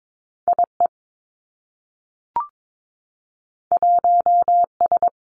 Morse code, heard as a keyed tone, about 22 wpm, 700 Hz. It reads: I E 1 S
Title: Two Challenging Letter-Number Combinations 22wpm